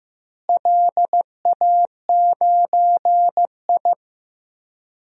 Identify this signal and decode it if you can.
Morse code, heard as a keyed tone, about 16 wpm, 700 Hz. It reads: LA9I
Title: DXCC-100 Call Signs - 4 Characters - Repeated 3x - 15wpm